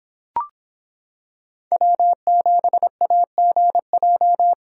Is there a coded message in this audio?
Yes: Morse code from 1.7 to 4.5 s, heard as a keyed tone, about 26 wpm, 700 Hz. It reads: W7AGJ